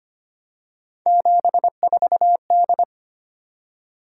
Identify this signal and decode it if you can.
Morse code, heard as a keyed tone, about 25 wpm, 700 Hz. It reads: 74D
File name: Three Letter-Number Combinations - 25wpm